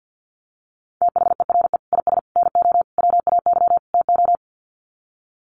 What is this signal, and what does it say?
Morse code, heard as a keyed tone, about 48 wpm, 700 Hz. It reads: THERE IS NO WAY TO
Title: Sets of 5 Words Encore 50wpm